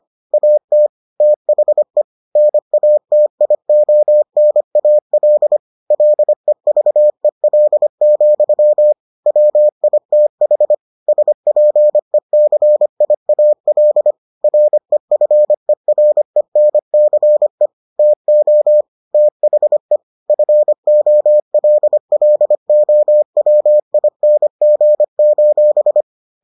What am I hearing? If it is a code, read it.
Morse code, heard as a keyed tone, about 25 wpm, 600 Hz. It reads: AT THE NATIONAL LEVEL, WITH SPECIAL REFERENCE TO THE FOLLOWING: